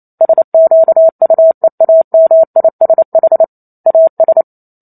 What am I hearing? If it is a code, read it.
Morse code, heard as a keyed tone, about 28 wpm, 650 Hz. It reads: SQUEAMISH AS